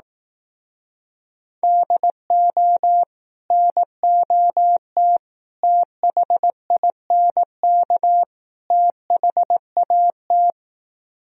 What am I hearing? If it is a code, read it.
Morse code, heard as a keyed tone, about 18 wpm, 700 Hz. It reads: DO NOT THINK THAT
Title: Speed-Racing - Sets of 4 Words 15wpm